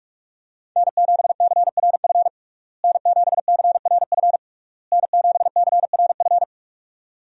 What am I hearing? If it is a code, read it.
Morse code, heard as a keyed tone, about 44 wpm, 700 Hz. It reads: N7CRF N7CRF N7CRF
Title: USA Call Signs - 5 Characters - Repeated 3x - 45wpm